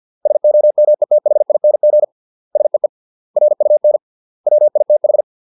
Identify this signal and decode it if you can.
Morse code, heard as a keyed tone, about 50 wpm, 600 Hz. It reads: SOMETHING SEE RUN WITH
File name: 632 words / 632 Words 50WPM